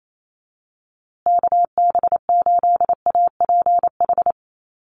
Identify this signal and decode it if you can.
Morse code, heard as a keyed tone, about 28 wpm, 690 Hz. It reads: KB8APH
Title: USA Call Signs Repeated 3x 28wpm